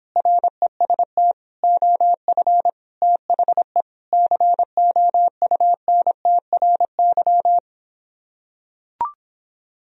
Morse code, heard as a keyed tone, about 26 wpm, 700 Hz. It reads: REST OF THE COUNTRY